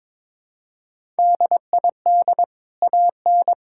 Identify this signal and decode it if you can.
Morse code, heard as a keyed tone, about 22 wpm, 700 Hz. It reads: DID AN
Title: Sentences from Top 1000 Words - Review Part 6 - 22wpm